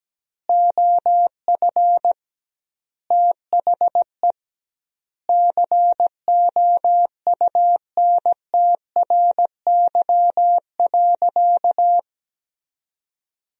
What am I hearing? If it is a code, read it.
Morse code, heard as a keyed tone, about 17 wpm, 700 Hz. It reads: OF THE COUNTRY.
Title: 2x Word-Spacing - Sentences from Top 300 Words - 17wpm